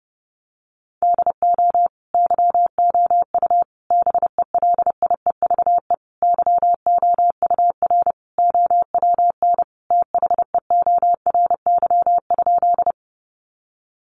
Morse code, heard as a keyed tone, about 30 wpm, 700 Hz. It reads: DO YOU BELIEVE YOUR OWN THEORY?
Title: Sentences from Top 1000 Words 30wpm